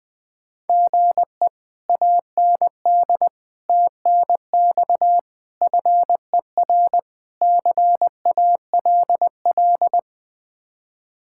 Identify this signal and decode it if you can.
Morse code, heard as a keyed tone, about 20 wpm, 700 Hz. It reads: GE AND TNX FER CALL
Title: Speed-Racing Common QSO Phrases - 17wpm